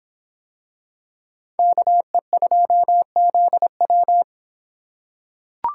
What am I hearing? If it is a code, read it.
Morse code, heard as a keyed tone, about 25 wpm, 700 Hz. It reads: KE2ZW